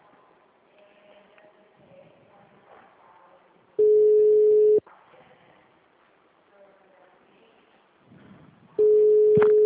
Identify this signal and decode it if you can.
Ringback tone (Europe, 425 Hz), with speech.